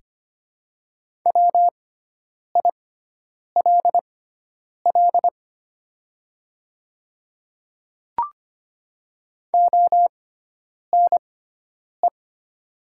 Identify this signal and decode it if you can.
Morse code, heard as a keyed tone, about 25 wpm, spaced as for 9 wpm, 700 Hz. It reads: WILL ONE
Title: 6x Character Spacing - Top 200 Words - 25wpm